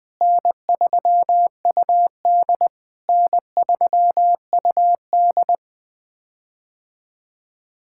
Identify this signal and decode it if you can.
Morse code, heard as a keyed tone, about 21 wpm, 700 Hz. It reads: N3UD N3UD
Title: POTA QSO Practice - 20wpm